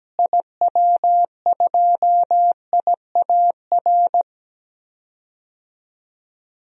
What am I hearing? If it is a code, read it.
Morse code, heard as a keyed tone, about 17 wpm, 700 Hz. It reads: IW2IAR
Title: DXCC-100 Call Signs - 6 Characters - Repeated 2x - 17wpm